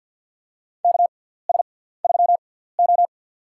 Morse code, heard as a keyed tone, about 47 wpm, spaced as for 17 wpm, 700 Hz. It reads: KS3O
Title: USA Call Signs - 4 Characters - 6x Character Spacing - 50wpm